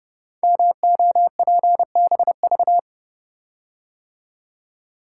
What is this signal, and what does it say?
Morse code, heard as a keyed tone, about 30 wpm, 700 Hz. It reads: MOPBV
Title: Code Groups - 5 Letters - 30wpm